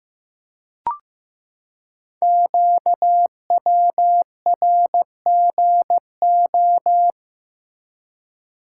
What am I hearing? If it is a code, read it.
Morse code, heard as a keyed tone, about 16 wpm, 700 Hz. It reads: QWRGO
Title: Code Groups - 5 Letters - 15wpm